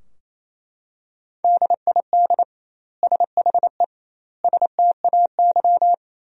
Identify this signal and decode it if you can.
Morse code, heard as a keyed tone, about 28 wpm, 700 Hz. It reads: DID SHE STAY
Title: 2x Word-Spacing - Sentences from Top 500 Words - 28wpm